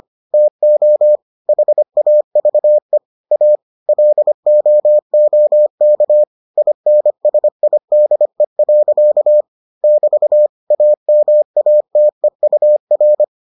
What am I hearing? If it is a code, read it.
Morse code, heard as a keyed tone, about 25 wpm, 600 Hz. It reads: TO HAVE A LOOK INSIDE. <BT> AMATEUR